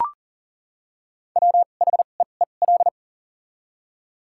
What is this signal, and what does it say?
Morse code, heard as a keyed tone, about 40 wpm, spaced as for 29 wpm, 690 Hz. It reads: WHEEL
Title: 2x Character Spacing - Top 500 Words - 40wpm